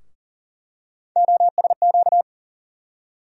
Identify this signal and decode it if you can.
Morse code, heard as a keyed tone, about 40 wpm, 700 Hz. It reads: OSQ
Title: USA Call Sign Postfixes - 3 Letters 40wpm